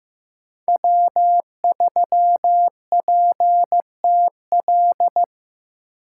Morse code, heard as a keyed tone, about 18 wpm, 700 Hz. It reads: W3PTL
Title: Speed Racing - USA 1x3 Call Signs 15wpm